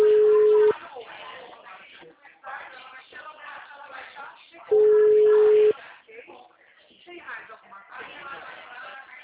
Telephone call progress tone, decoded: ringback tone

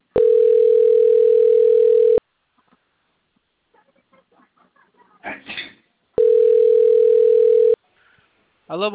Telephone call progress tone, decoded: ringback tone

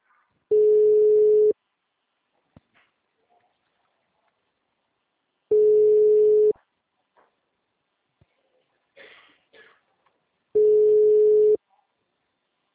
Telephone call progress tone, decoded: ringback tone

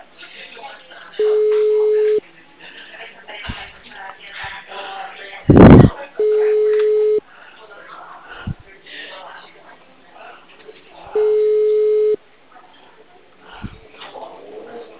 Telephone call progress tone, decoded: ringback tone

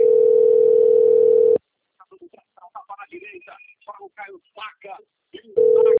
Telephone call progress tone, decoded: ringback tone

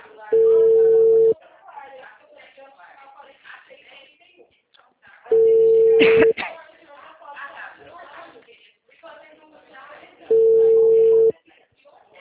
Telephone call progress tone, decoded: ringback tone

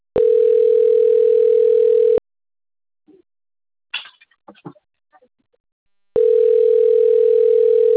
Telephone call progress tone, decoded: ringback tone

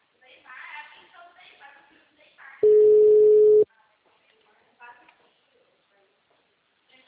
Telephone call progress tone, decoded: ringback tone

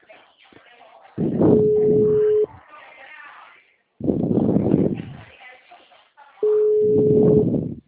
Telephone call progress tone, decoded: ringback tone

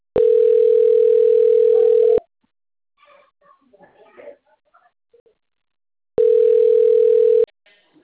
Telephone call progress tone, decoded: ringback tone

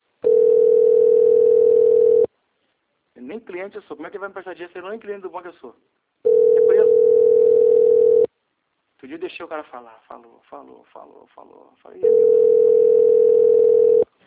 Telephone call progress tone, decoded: ringback tone

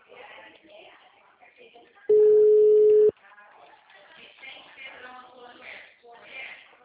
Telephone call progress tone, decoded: ringback tone

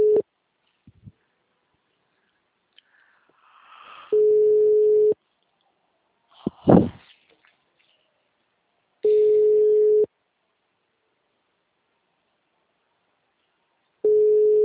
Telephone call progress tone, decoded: ringback tone

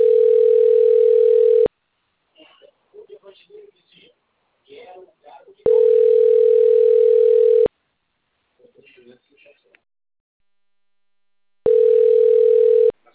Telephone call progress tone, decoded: ringback tone

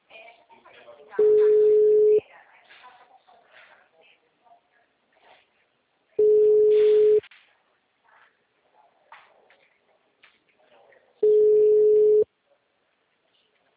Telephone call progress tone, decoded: ringback tone

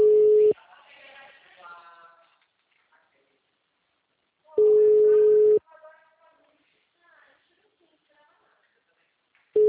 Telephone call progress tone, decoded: ringback tone

